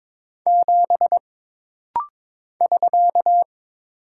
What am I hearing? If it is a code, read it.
Morse code, heard as a keyed tone, about 22 wpm, 700 Hz. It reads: 7 <SK>
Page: This rapid-fire timing is geared to push your ICR (Instant Character Recognition) skills.